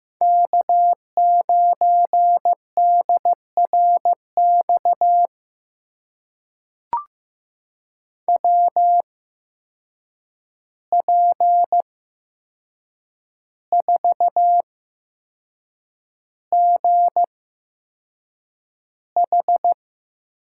Morse code, heard as a keyed tone, about 15 wpm, spaced as for 6 wpm, 700 Hz. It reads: K9DRX WP4GH
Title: USA Call Signs - 8x Character Spacing - 15wpm